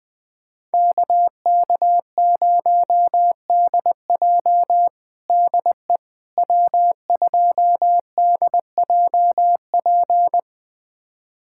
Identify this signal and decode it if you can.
Morse code, heard as a keyed tone, about 20 wpm, 700 Hz. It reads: KK0DJ DE W2DJP
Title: USA Callsign DE Callsign - 5 Characters - 20wpm